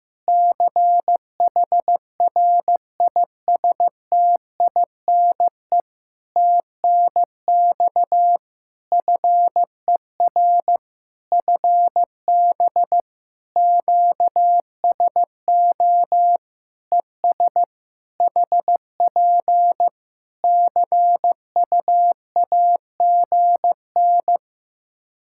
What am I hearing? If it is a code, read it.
Morse code, heard as a keyed tone, about 15 wpm, 700 Hz. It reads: CHRISTINE TNX FER FB QSO ES HP CUAGN
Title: Practice QSO - Exchange 4 - Ending - 15wpm